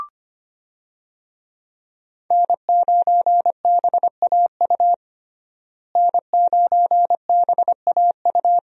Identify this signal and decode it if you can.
Morse code, heard as a keyed tone, about 25 wpm, 700 Hz. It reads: N9BAU N9BAU
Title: USA Call Signs - 5 Characters - Repeated 3x - 25wpm